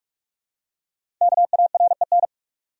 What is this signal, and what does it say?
Morse code, heard as a keyed tone, about 46 wpm, 700 Hz. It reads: KAREN
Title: USA Common Names - 45wpm